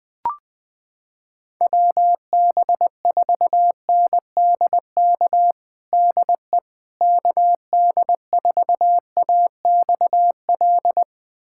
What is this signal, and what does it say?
Morse code, heard as a keyed tone, about 21 wpm, 700 Hz. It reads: WB4NDK DE KD4AXL
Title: USA Callsign DE Callsign - 6 Characters - 20wpm